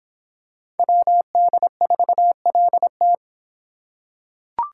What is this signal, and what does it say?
Morse code, heard as a keyed tone, about 25 wpm, 700 Hz. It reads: WD4LT